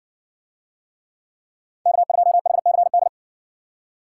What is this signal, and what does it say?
Morse code, heard as a keyed tone, about 54 wpm, 700 Hz. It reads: K2HCD